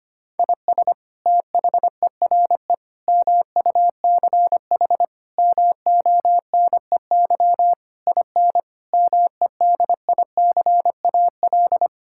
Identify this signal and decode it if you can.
Morse code, heard as a keyed tone, about 25 wpm, 700 Hz. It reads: IS THERE MUCH MONEY IN MEDICAL
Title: Sentences from Top 1000 Words - Review Part 5 - 25wpm